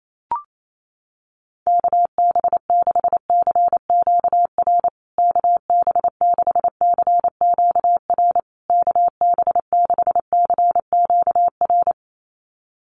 Morse code, heard as a keyed tone, about 28 wpm, 700 Hz. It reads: KB6CQR KB6CQR KB6CQR
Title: USA Call Signs Repeated 3x 28wpm